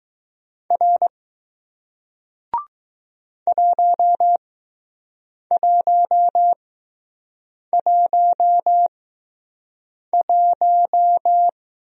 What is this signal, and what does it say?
Morse code, heard as a keyed tone, about 20 wpm, spaced as for 6 wpm, 700 Hz. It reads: R 1111